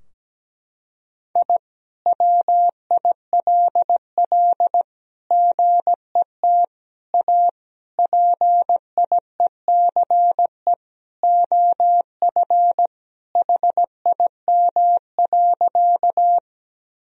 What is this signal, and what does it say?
Morse code, heard as a keyed tone, about 17 wpm, 700 Hz. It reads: I WILL GET A PIECE OF HIM.
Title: Sentences from Top 1000 Words - Review Part 2 - 17wpm